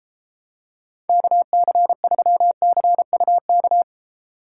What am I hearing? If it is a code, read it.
Morse code, heard as a keyed tone, about 33 wpm, 700 Hz. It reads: KC3CUK